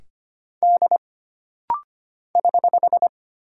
Morse code, heard as a keyed tone, about 25 wpm, 700 Hz. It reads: D <HH>
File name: Single Character (Letter:Number:Punctuation:Prosign) - Rapid-Fire - 25wpm